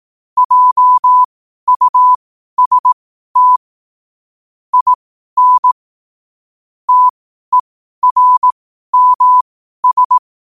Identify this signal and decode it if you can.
Morse code, heard as a keyed tone, about 18 wpm, spaced as for 12 wpm, 1000 Hz. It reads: JUST IN TERMS
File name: qotdfiles / 12wpm